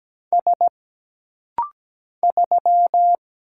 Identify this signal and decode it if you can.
Morse code, heard as a keyed tone, about 17 wpm, 700 Hz. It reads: S 3
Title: Single Letter-Number - Rapid-Fire 17wpm